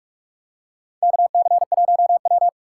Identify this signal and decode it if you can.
Morse code, heard as a keyed tone, about 43 wpm, 700 Hz. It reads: KC1W
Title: USA Call Signs - 2x Character Spacing - 45wpm